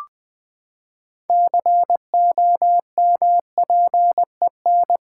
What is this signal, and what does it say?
Morse code, heard as a keyed tone, about 20 wpm, 700 Hz. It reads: COMPEN
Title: Build A Word - 6 Letters At A Time - 12-Letter Words - 20wpm